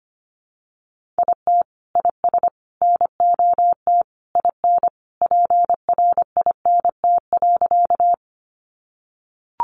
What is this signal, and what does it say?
Morse code, heard as a keyed tone, about 25 wpm, 700 Hz. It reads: IT IS NOT IN PRINT.
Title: Sentences from Top 1400 Words 25wpm